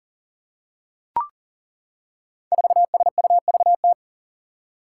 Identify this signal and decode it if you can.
Morse code, heard as a keyed tone, about 40 wpm, 700 Hz. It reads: 4SUVT
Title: Five Challenging Letter-Number Combinations 40wpm